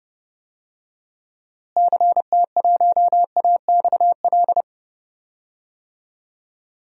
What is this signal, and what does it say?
Morse code, heard as a keyed tone, about 30 wpm, 700 Hz. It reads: CT1AXL
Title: DXCC-100 Call Signs - 6 Characters - Repeated 3x - 30wpm